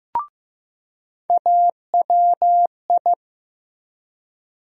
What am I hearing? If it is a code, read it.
Morse code, heard as a keyed tone, about 15 wpm, 700 Hz. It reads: AWI
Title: USA Call Sign Postfixes - 3 Letters 15wpm